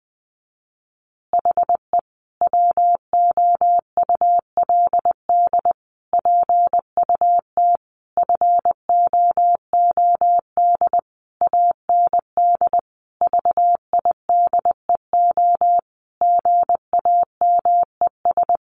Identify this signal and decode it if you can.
Morse code, heard as a keyed tone, about 20 wpm, 700 Hz. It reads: HE WOULD PUT FOOD AND VIDEO GAMES
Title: Easy Phrases 20wpm